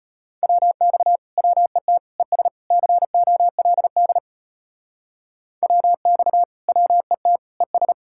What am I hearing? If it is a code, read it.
Morse code, heard as a keyed tone, about 31 wpm, 700 Hz. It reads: WX WET ES COLD WX WET ES